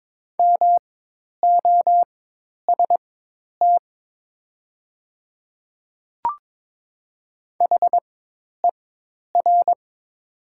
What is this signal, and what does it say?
Morse code, heard as a keyed tone, about 22 wpm, spaced as for 10 wpm, 700 Hz. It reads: MOST HER
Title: 4x Character Spacing - Top 100 Words - 22wpm